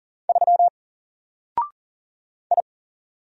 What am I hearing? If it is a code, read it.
Morse code, heard as a keyed tone, about 40 wpm, 700 Hz. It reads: 3 I
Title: Single Letter-Number Alt 40wpm